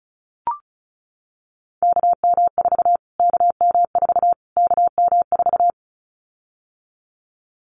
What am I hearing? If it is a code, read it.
Morse code, heard as a keyed tone, about 34 wpm, 700 Hz. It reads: KM4 KM4 KM4
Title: Callsign Prefixes for Top 25 DXCC - Repeated 3x - 35wpm